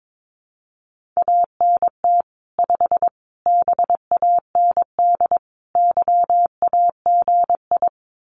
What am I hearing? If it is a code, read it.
Morse code, heard as a keyed tone, about 22 wpm, 700 Hz. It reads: ANT 5 BAND YAGI
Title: Common QSO Phrases 22wpm